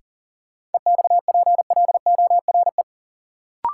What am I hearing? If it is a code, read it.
Morse code, heard as a keyed tone, about 40 wpm, 700 Hz. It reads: EXPLORE
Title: Sentences from Top 1100 Words - New Word Drill - 40wpm